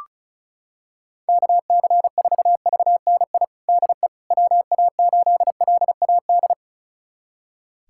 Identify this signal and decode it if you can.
Morse code, heard as a keyed tone, about 36 wpm, 700 Hz. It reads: KC4VNI DE WA8LAD
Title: USA Callsign DE Callsign 35wpm